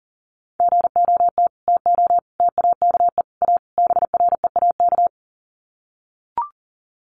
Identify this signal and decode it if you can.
Morse code, heard as a keyed tone, about 40 wpm, 700 Hz. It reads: GOT TO TAKE A BREAK
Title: Sets of 5 Words Encore 40wpm